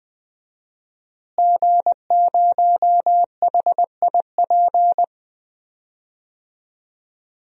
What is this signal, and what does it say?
Morse code, heard as a keyed tone, about 20 wpm, 700 Hz. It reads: G0HIP
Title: DXCC-100 Call Signs - 5 Characters - Repeated 3x - 20wpm